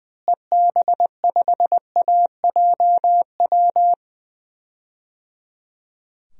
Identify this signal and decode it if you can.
Morse code, heard as a keyed tone, about 21 wpm, 700 Hz. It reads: EB5AJW
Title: DXCC-100 Call Signs - 6 Characters - Repeated 3x - 20wpm